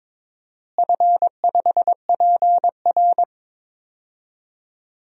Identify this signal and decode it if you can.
Morse code, heard as a keyed tone, about 22 wpm, 700 Hz. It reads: F5PR
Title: DXCC-100 Call Signs - 4 Characters - Repeated 3x - 22wpm